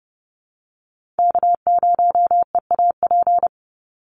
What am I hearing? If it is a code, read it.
Morse code, heard as a keyed tone, about 30 wpm, 700 Hz. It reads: K0EAP
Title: NA SOTA Chaser Callsigns 30wpm